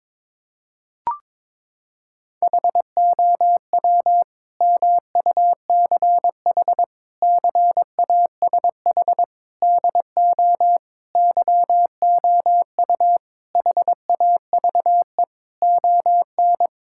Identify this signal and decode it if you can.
Morse code, heard as a keyed tone, about 21 wpm, 700 Hz. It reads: HOW MUCH CASH DO YOU HAVE ON
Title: Sentences from Top 1000 Words - Review Part 5 - 22wpm